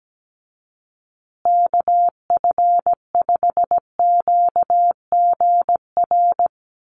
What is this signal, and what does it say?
Morse code, heard as a keyed tone, about 17 wpm, 700 Hz. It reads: KF5QGR
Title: USA Call Signs Repeated 3x 17wpm